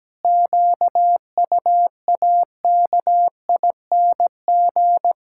Morse code, heard as a keyed tone, about 17 wpm, 700 Hz. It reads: QUAKING
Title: Pangram Challenge v2 - 17wpm